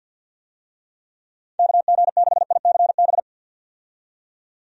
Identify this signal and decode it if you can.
Morse code, heard as a keyed tone, about 52 wpm, 700 Hz. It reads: KG6ICB
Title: USA Call Signs - 6 Characters - 50wpm